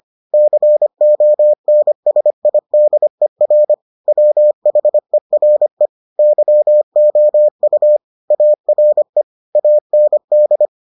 Morse code, heard as a keyed tone, about 25 wpm, 600 Hz. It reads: CONSIDER WHERE YOU ARE AND